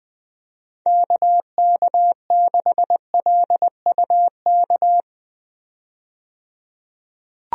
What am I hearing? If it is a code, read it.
Morse code, heard as a keyed tone, about 21 wpm, 700 Hz. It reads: KK6LUK
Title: USA Call Signs - 6 Characters - 20wpm